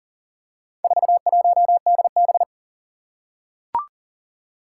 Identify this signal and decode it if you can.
Morse code, heard as a keyed tone, about 38 wpm, 700 Hz. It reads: 41DB
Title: Four Challenging Letter-Number Combinations 40wpm